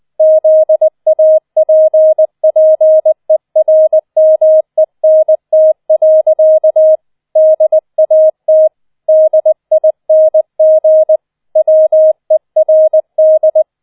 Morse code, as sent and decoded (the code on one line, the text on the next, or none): --.. .- .--. .--. .-. -- . -. - .-.-.- / -.. .- - / -.. .. -. --. / .-- . .-. -..
ZAPPERMENT. DAT DING WERD